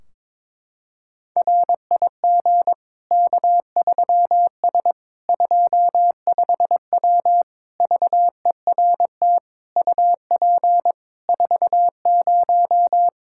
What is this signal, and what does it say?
Morse code, heard as a keyed tone, about 22 wpm, 700 Hz. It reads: RIG K3S 25W VERT UP 40
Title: K1USN SST QSO Practice 22wpm